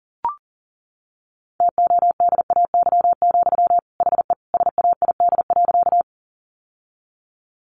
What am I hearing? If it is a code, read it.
Morse code, heard as a keyed tone, about 40 wpm, 700 Hz. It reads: TODAY, HE SAID.
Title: Sentences from Top 600 Words 40wpm